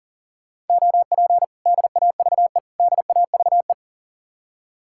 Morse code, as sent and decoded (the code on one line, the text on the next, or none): --- .--. / -.. .- ...- . / -.. .- ...- .
OP DAVE DAVE